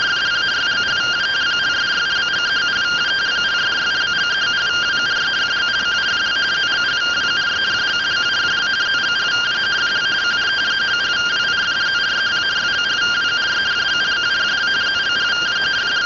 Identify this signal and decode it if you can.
Two-tone teleprinter signal, about 170 Hz shift.
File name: findings / axial iodine 10mm 184.gen-40 rtty rx